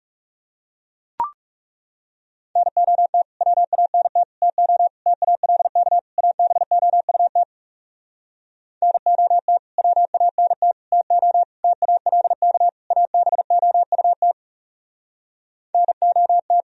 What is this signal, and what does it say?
Morse code, heard as a keyed tone, about 40 wpm, 700 Hz. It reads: NOT WANT TO TALK ABOUT NOT WANT TO TALK ABOUT NOT